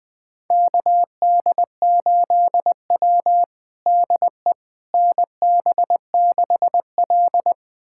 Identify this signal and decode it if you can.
Morse code, heard as a keyed tone, about 21 wpm, 700 Hz. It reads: KD8W DE NB6L
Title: USA Callsign DE Callsign - 4 Characters - 20wpm